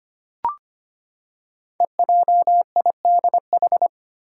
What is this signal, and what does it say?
Morse code, heard as a keyed tone, about 25 wpm, 700 Hz. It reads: EJIDH
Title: Code Groups - 5 Letters - 25wpm